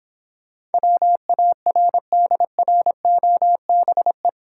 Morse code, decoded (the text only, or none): WARDROBE